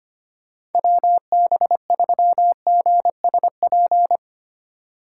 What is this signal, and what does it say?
Morse code, heard as a keyed tone, about 32 wpm, 700 Hz. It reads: WB3GSP